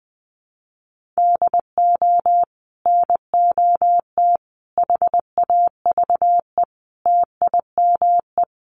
Morse code, heard as a keyed tone, about 20 wpm, 700 Hz. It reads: DO NOT HAVE TIME
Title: Sets of 5 Words 20wpm